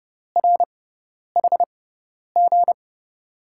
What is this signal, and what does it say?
Morse code, heard as a keyed tone, about 30 wpm, spaced as for 10 wpm, 700 Hz. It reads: RHG
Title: USA Call Signs - 5 Characters - 6x Character Spacing - 30wpm